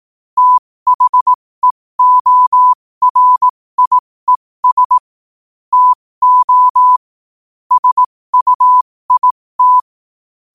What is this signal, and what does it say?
Morse code, heard as a keyed tone, about 18 wpm, spaced as for 15 wpm, 1000 Hz. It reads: THEORIES TO SUIT
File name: qotdfiles / 15wpm